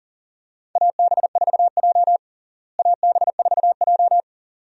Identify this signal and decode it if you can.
Morse code, heard as a keyed tone, about 37 wpm, 700 Hz. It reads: AB4J AB4J
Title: USA Call Signs - 4 Characters - Repeated 2x - 40wpm